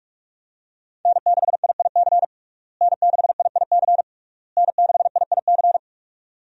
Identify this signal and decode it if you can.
Morse code, heard as a keyed tone, about 46 wpm, 700 Hz. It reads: N6IIC N6IIC N6IIC
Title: USA Call Signs - 5 Characters - Repeated 3x - 45wpm